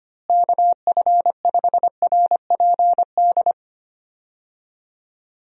Morse code, heard as a keyed tone, about 25 wpm, 700 Hz. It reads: KF5RPD
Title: USA Call Signs - 6 Characters - Repeated 3x - 25wpm